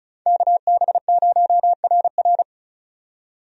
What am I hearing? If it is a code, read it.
Morse code, heard as a keyed tone, about 34 wpm, 700 Hz. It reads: KB0RR
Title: USA Call Signs - 5 Characters - 35wpm